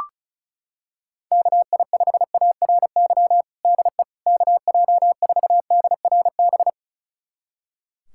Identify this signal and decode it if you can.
Morse code, heard as a keyed tone, about 36 wpm, 700 Hz. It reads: KI5ARY DE KJ4DRB
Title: USA Callsign DE Callsign 35wpm